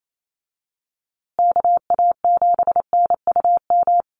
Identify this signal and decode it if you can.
Morse code, heard as a keyed tone, about 28 wpm, 700 Hz. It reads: KA7NUM
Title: USA Call Signs Repeated 2x 28wpm